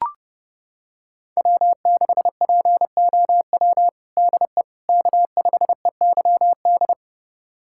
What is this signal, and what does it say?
Morse code, heard as a keyed tone, about 30 wpm, 700 Hz. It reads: W6POW DE K5EYD
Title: USA Callsign DE Callsign 30wpm